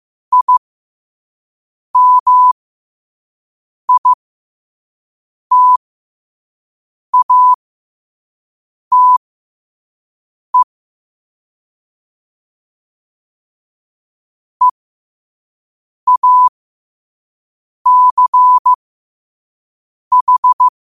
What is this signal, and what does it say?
Morse code, heard as a keyed tone, about 15 wpm, spaced as for 5 wpm, 1000 Hz. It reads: IMITATE EACH
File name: qotdfiles / 5wpm